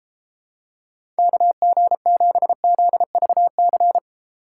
Morse code, heard as a keyed tone, about 31 wpm, 700 Hz. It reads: KG7ZVC